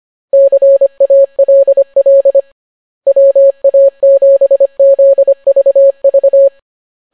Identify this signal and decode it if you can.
Morse code, heard as a keyed tone, about 25 wpm, 550 Hz. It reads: CALL WA7ZVV